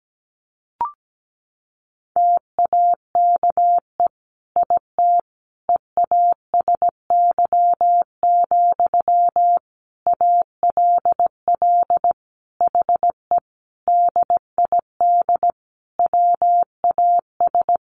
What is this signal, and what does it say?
Morse code, heard as a keyed tone, about 17 wpm, 700 Hz. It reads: TAKE IT EASY, ALL HE DID WAS
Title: A Sound Of Thunder 17wpm